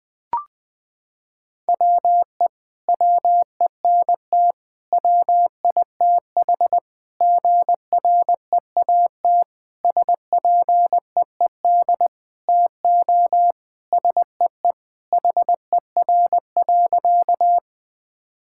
Morse code, heard as a keyed tone, about 20 wpm, 700 Hz. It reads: WE WENT WITH GREAT SPEED TO SEE HER.
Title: Sentences from Top 1000 Words - Review Part 5 - 20wpm